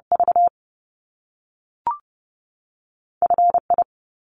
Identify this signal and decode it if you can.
Morse code, heard as a keyed tone, about 30 wpm, 700 Hz. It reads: V F I
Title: Top Two Letter Combinations in Words 30wpm